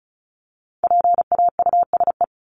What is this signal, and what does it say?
Morse code, heard as a keyed tone, about 34 wpm, 700 Hz. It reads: PAUSE
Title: Top 1501-1750 Words 35wpm